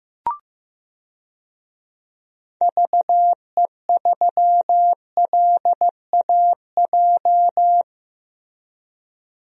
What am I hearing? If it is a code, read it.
Morse code, heard as a keyed tone, about 15 wpm, 700 Hz. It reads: VE3LAJ
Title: DXCC-100 Call Signs - 6 Characters - Repeated 2x - 15wpm